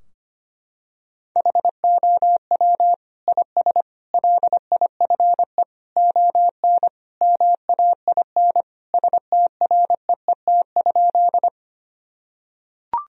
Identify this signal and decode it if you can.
Morse code, heard as a keyed tone, about 25 wpm, 700 Hz. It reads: HOW IS LIFE ON MAIN STREET?
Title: Sentences from Top 1000 Words - Review Part 2 - 25wpm